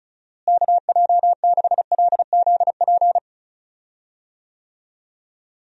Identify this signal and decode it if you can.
Morse code, heard as a keyed tone, about 36 wpm, 700 Hz. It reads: KJ6LZP
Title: USA Call Signs - 6 Characters - Repeated 2x - 35wpm